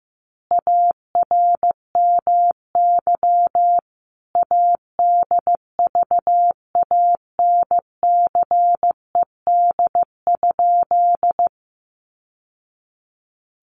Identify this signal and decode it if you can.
Morse code, heard as a keyed tone, about 15 wpm, 700 Hz. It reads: ARMY ADVANCED?
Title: Sentences from Top 1200 Words 15wpm